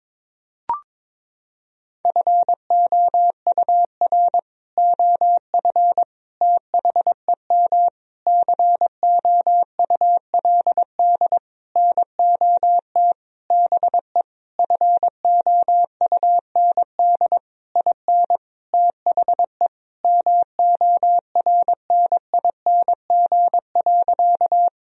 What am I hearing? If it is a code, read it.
Morse code, heard as a keyed tone, about 22 wpm, 700 Hz. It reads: FOUR OF THEM COULD NOT BE FOUND IN THE MORNING.
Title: Sentences from Top 1000 Words - Review Part 1 - 22wpm